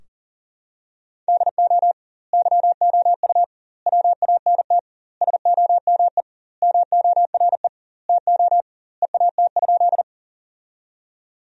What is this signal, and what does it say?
Morse code, heard as a keyed tone, about 40 wpm, 700 Hz. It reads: DO YOU WANT SOME MORE TO EAT?
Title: 2x Word-Spacing - Sentences from Top 300 Words - 40wpm